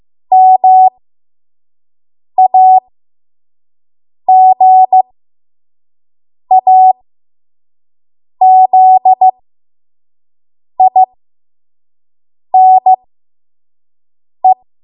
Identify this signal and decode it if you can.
Morse code, heard as a keyed tone, about 15 wpm, spaced as for 5 wpm, 750 Hz. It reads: MAGAZINE
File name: Morse / 05WPM